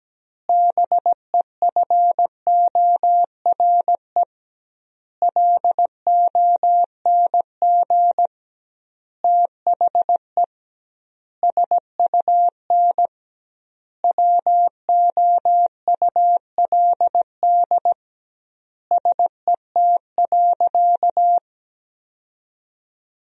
Morse code, heard as a keyed tone, about 17 wpm, 700 Hz. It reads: BEFORE LONG THE SUN WOULD SET.
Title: 2x Word-Spacing - Sentences from Top 400 Words - 17wpm